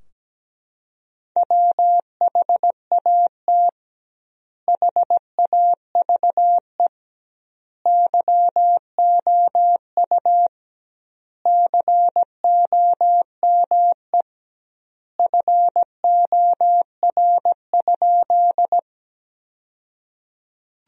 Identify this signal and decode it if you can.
Morse code, heard as a keyed tone, about 17 wpm, 700 Hz. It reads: WHAT HAVE YOU COME FOR?
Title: 2x Word-Spacing - Sentences from Top 100 Words - 17wpm